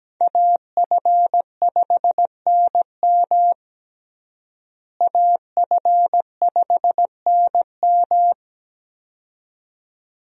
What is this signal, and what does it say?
Morse code, heard as a keyed tone, about 17 wpm, 700 Hz. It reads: AF5NM AF5NM
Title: USA Call Signs - 5 Characters - Repeated 2x - 17wpm